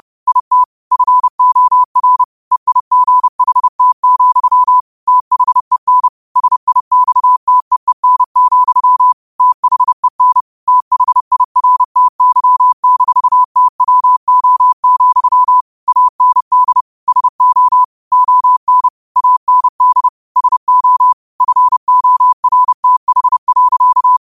Morse code, heard as a keyed tone, about 30 wpm, 1000 Hz. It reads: IT FOR EIGHT, THEN SIXTEEN, THEN THIRTY-TWO, AND SO ON AND SO FORTH.
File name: qotdfiles / 30wpm